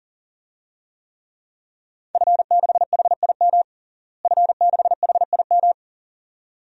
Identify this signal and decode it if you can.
Morse code, heard as a keyed tone, about 40 wpm, 700 Hz. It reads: F6HIM F6HIM
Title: DXCC-100 Call Signs - 5 Characters - Repeated 2x - 40wpm